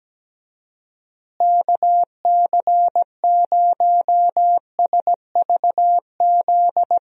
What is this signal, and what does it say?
Morse code, heard as a keyed tone, about 17 wpm, 700 Hz. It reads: KC0SVZ
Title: USA Call Signs - 6 Characters - Repeated 3x - 17wpm